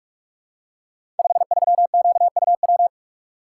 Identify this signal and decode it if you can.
Morse code, heard as a keyed tone, about 44 wpm, 700 Hz. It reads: 53QUW
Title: Code Groups - 5 Letter:Numbers - 45wpm